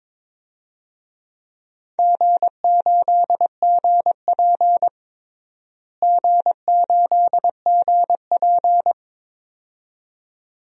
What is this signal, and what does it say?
Morse code, heard as a keyed tone, about 22 wpm, 700 Hz. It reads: G8GP G8GP
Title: DXCC-100 Call Signs - 4 Characters - Repeated 2x - 22wpm